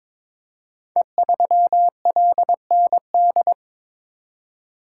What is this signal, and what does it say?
Morse code, heard as a keyed tone, about 23 wpm, 690 Hz. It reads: E3LND
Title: Code Groups - 5 Letter:Numbers - 22wpm